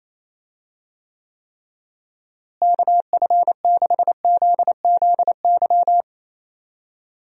Morse code, heard as a keyed tone, about 28 wpm, 700 Hz. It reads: KF6ZZY
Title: USA Call Signs - 6 Characters - Repeated 3x - 28wpm